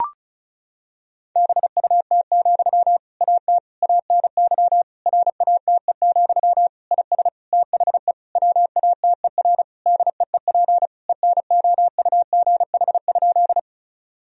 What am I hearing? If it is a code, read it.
Morse code, heard as a keyed tone, about 36 wpm, 700 Hz. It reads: BUT, AT ANY RATE, IS THE WATER DEEP ENOUGH?
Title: Sentences from Top 1000 Words - Review Part 4 - 35wpm